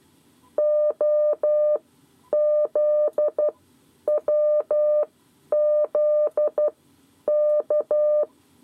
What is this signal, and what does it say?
Morse code, heard as a keyed tone, about 12 wpm, spaced as for 9 wpm, 600 Hz. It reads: OZWZK
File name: 7 Lezione cw 2